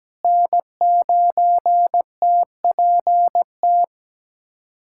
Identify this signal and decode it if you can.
Morse code, heard as a keyed tone, about 17 wpm, 700 Hz. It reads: N9TPT
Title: USA Callsign DE Callsign 17wpm